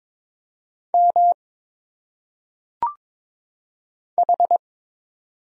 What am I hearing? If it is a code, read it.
Morse code, heard as a keyed tone, about 22 wpm, 700 Hz. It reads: M H
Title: Single Letters - Phonetic Pronunciation 22wpm